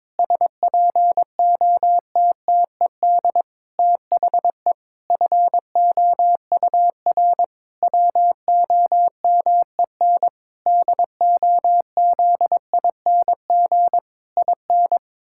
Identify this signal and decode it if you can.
Morse code, heard as a keyed tone, about 22 wpm, 700 Hz. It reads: SPOTTED THE FOUR WOMEN DOZING IN
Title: Pangram Challenge v2 - 22wpm